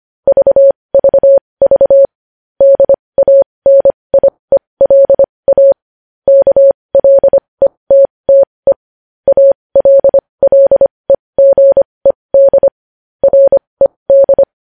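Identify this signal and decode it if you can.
Morse code, heard as a keyed tone, about 25 wpm, spaced as for 20 wpm, 550 Hz. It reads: VVV DANIELA KLETTE ALLEGED RED
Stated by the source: Morse code news headlines sent at 20 wpm.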